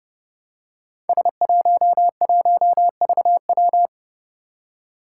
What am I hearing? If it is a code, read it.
Morse code, heard as a keyed tone, about 30 wpm, 700 Hz. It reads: S11VW
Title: Five Challenging Letter-Number Combinations 30wpm